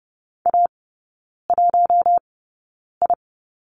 Morse code, heard as a keyed tone, about 30 wpm, spaced as for 9 wpm, 700 Hz. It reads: A1I
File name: 081 Lesson 14e - Number 2 - Three Characters 30wpm